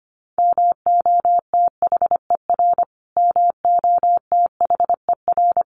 Morse code, heard as a keyed tone, about 25 wpm, 700 Hz. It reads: MOTHER MOTHER
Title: Top 300 Words Repeated 25wpm